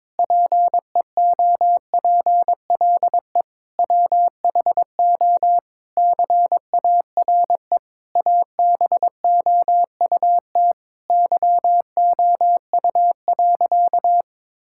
Morse code, heard as a keyed tone, about 22 wpm, 700 Hz. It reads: PEOPLE WHO CARE ABOUT YOU.
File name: Sentences with 7-Word Phrases - 22wpm